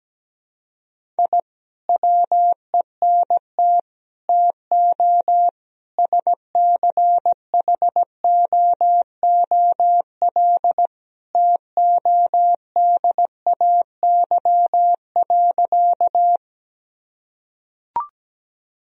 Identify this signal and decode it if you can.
Morse code, heard as a keyed tone, about 17 wpm, 700 Hz. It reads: I WENT TO SCHOOL TODAY.
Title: Sentences from Top 1000 Words - Review Part 1 - 17wpm